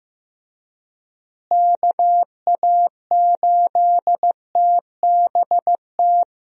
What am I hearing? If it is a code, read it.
Morse code, heard as a keyed tone, about 15 wpm, 700 Hz. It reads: KA8TBT
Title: USA Call Signs - 6 Characters - Repeated 2x - 15wpm